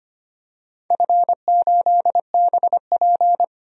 Morse code, heard as a keyed tone, about 25 wpm, 700 Hz. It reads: F8BP
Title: DXCC-100 Call Signs - 4 Characters - 25wpm